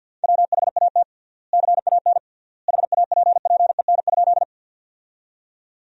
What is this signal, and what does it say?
Morse code, heard as a keyed tone, about 52 wpm, 700 Hz. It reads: WHAT CAN HAPPEN?
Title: 3x Word-Spacing - Sentences from Top 400 Words - 50wpm